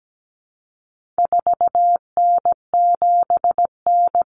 Morse code, heard as a keyed tone, about 17 wpm, 700 Hz. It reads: 4N7N
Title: Common Cut Numbers - 4-Digit 17wpm